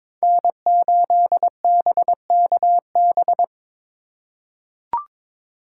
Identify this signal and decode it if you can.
Morse code, heard as a keyed tone, about 22 wpm, 700 Hz. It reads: N8BKB
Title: USA Call Signs - 5 Characters - 3x Character Spacing - 22wpm